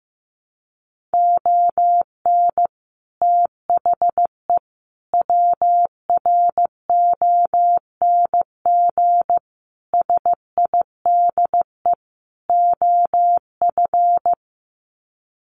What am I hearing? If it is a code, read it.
Morse code, heard as a keyed tone, about 15 wpm, 700 Hz. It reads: ON THE WRONG SIDE OF
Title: Sets of 5 Words Encore 15wpm